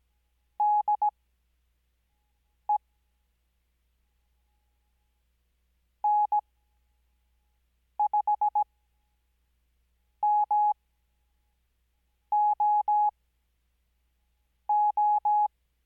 Morse code, heard as a keyed tone, about 17 wpm, spaced as for 5 wpm, 840 Hz. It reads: DE N5MOO
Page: Farnsworth Spacing: 15WPM Character speed Spaced at 5WPM.